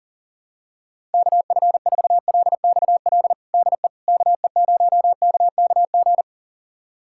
Morse code, heard as a keyed tone, about 40 wpm, 700 Hz. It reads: KF4LXL DE KE0KKG
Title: USA Callsign DE Callsign - 6 Characters - 40wpm